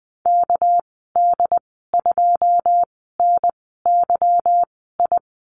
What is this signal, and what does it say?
Morse code, heard as a keyed tone, about 20 wpm, spaced as for 14 wpm, 700 Hz. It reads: KD2NYI
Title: 213 Lesson 34g - Number 8 - Call Signs 20wpm